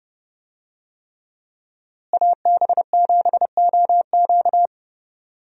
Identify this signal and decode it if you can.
Morse code, heard as a keyed tone, about 30 wpm, 700 Hz. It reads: AB7OQ
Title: USA Call Signs - 5 Characters - Repeated 3x - 30wpm